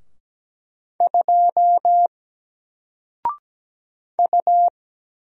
Morse code, heard as a keyed tone, about 17 wpm, 700 Hz. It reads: 2 U
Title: Single Letter-Number Alt 17wpm